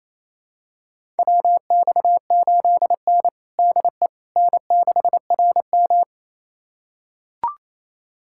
Morse code, heard as a keyed tone, about 27 wpm, 690 Hz. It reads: WX8N DE N6RM
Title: USA Callsign DE Callsign - 4 Characters - 28wpm